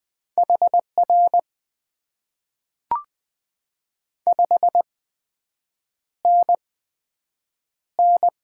Morse code, heard as a keyed tone, about 20 wpm, spaced as for 7 wpm, 700 Hz. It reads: HR 5NN
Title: Top 90 QSO Elements - 8x Character Spacing - 20wpm